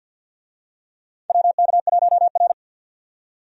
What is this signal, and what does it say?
Morse code, heard as a keyed tone, about 52 wpm, 700 Hz. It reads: WK1R
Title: USA Call Signs - 4 Characters - 50wpm